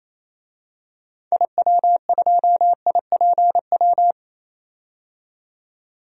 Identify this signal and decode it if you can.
Morse code, heard as a keyed tone, about 28 wpm, 700 Hz. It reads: IW2IPW
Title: DXCC-100 Call Signs - 6 Characters - 28wpm